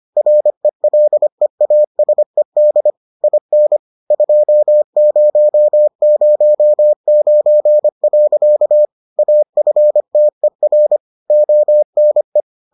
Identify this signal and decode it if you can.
Morse code, heard as a keyed tone, about 25 wpm, 600 Hz. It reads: RELEASED IN 2009. AFTER ONE